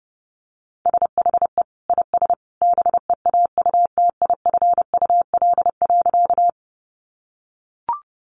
Morse code, heard as a keyed tone, about 30 wpm, 700 Hz. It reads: SHE IS BEAUTIFUL.
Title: Sentences from Top 900 Words 30wpm